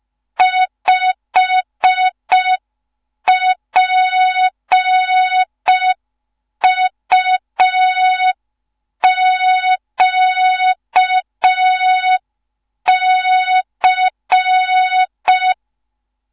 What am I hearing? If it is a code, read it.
Morse code, heard as a keyed tone, about 5 wpm, 3700 Hz. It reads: 5PUQC
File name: Morze-HA2MN / hang / 15 GYAKORLAT